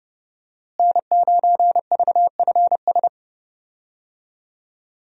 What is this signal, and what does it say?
Morse code, heard as a keyed tone, about 30 wpm, 700 Hz. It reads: N9VFS
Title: USA Call Signs - 5 Characters - Repeated 2x - 30wpm